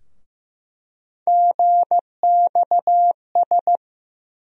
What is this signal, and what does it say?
Morse code, heard as a keyed tone, about 15 wpm, 700 Hz. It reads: GXS
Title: USA Call Sign Postfixes - 3 Letters 15wpm